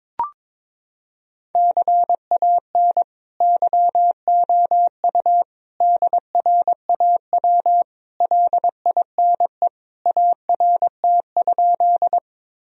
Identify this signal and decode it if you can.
Morse code, heard as a keyed tone, about 22 wpm, 700 Hz. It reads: CAN YOU DRAW LINE ART?
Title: Sentences from Top 1000 Words - Review Part 3 - 22wpm